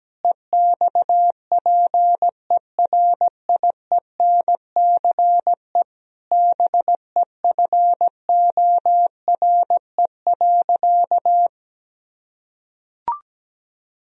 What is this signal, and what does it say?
Morse code, heard as a keyed tone, about 17 wpm, 700 Hz. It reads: EXPERIENCE BEFORE.
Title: Sentences from Top 1000 Words - Review Part 3 - 17wpm